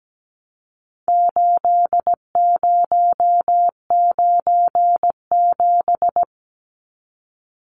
Morse code, heard as a keyed tone, about 17 wpm, 700 Hz. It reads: 8097
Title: Numbers - 4-Digit 17wpm